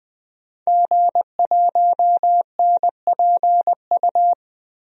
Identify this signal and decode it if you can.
Morse code, heard as a keyed tone, about 20 wpm, 700 Hz. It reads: G1NPU
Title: DXCC-100 Call Signs - 5 Characters - Repeated 2x - 20wpm